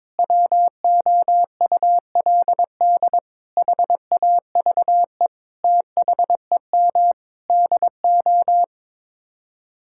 Morse code, heard as a keyed tone, about 22 wpm, 700 Hz. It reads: WOULD HAVE THEM DO
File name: Sets of 9 Words - 22wpm